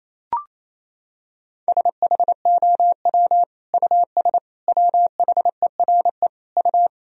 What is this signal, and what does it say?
Morse code, heard as a keyed tone, about 28 wpm, 700 Hz. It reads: SHOW US WHERE U
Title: Sentences from Top 200 Words and Abbreviations 28wpm